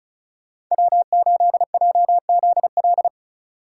Morse code, heard as a keyed tone, about 37 wpm, 700 Hz. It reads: W8JZL
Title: Speed Racing - USA 1x3 Call Signs 30wpm